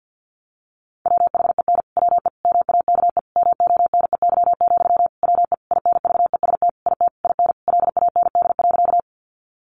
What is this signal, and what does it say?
Morse code, heard as a keyed tone, about 50 wpm, 700 Hz. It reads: WHEN WE MAKE MONEY, WE INVEST IT IN LAND.
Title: Sentences from Top 1100 Words 50wpm